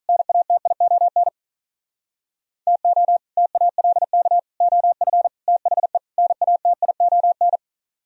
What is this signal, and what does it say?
Morse code, heard as a keyed tone, about 47 wpm, 700 Hz. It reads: NATION TO TALK OF THE NATION